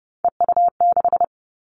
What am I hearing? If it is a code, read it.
Morse code, heard as a keyed tone, about 30 wpm, 700 Hz. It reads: EU6
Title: All Cut Numbers - 3-Digit 30wpm